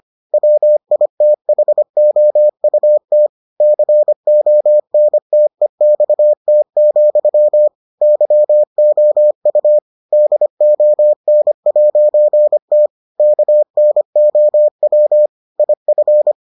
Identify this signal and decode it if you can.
Morse code, heard as a keyed tone, about 25 wpm, 600 Hz. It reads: WITHOUT CONTEXT, YOU DON'T KNOW IF